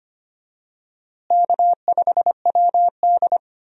Morse code, heard as a keyed tone, about 25 wpm, 700 Hz. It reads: K5WD
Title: USA Call Signs - 4 Characters - Repeated 3x - 25wpm